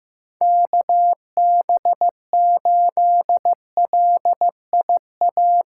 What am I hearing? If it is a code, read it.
Morse code, heard as a keyed tone, about 15 wpm, 700 Hz. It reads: KB8LIA
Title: USA Callsign DE Callsign - 6 Characters - 15wpm